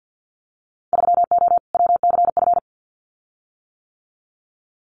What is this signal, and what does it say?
Morse code, heard as a keyed tone, about 50 wpm, 700 Hz. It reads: 3O WCF
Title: Winter Field Day Exchanges 50wpm